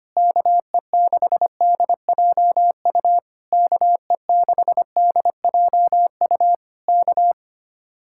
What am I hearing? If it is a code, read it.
Morse code, heard as a keyed tone, about 24 wpm, 700 Hz. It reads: KE6DJU KE6DJU K
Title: Speed-Racing Common QSO Phrases - 25wpm